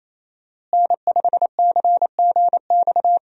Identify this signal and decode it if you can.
Morse code, heard as a keyed tone, about 28 wpm, 700 Hz. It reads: N5CGX
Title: USA Call Signs - 5 Characters - Repeated 3x - 28wpm